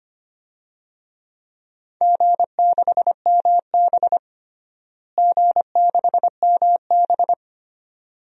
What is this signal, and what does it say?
Morse code, heard as a keyed tone, about 25 wpm, 700 Hz. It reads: G6MB G6MB
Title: DXCC-100 Call Signs - 4 Characters - Repeated 2x - 25wpm